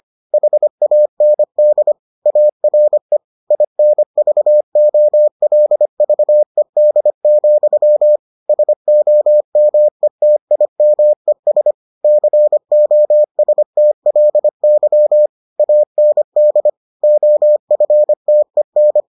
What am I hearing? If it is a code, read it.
Morse code, heard as a keyed tone, about 25 wpm, 600 Hz. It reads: HAND ARE INVOLVED, SOMETIMES COSTLY AND OFTEN